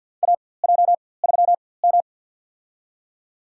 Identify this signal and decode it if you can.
Morse code, heard as a keyed tone, about 49 wpm, spaced as for 23 wpm, 700 Hz. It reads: AJ3M
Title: USA Call Signs - 4x Character Spacing - 50wpm